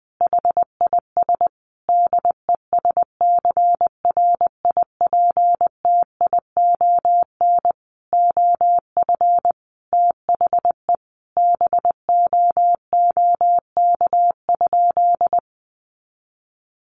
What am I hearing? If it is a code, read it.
Morse code, heard as a keyed tone, about 19 wpm, 700 Hz. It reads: HIS DESCRIPTION OF THE BOOK?
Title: Sentences from Top 1300 Words 20wpm